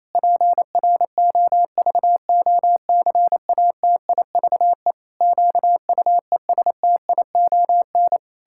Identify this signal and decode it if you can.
Morse code, heard as a keyed tone, about 28 wpm, 700 Hz. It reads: PROVOCATIVE QUESTION
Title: Pangram Challenge v2 - 28wpm